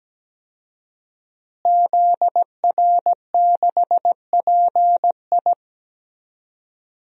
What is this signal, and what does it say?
Morse code, heard as a keyed tone, about 17 wpm, 700 Hz. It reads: ZR6PI
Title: DXCC-100 Call Signs - 5 Characters - Repeated 2x - 17wpm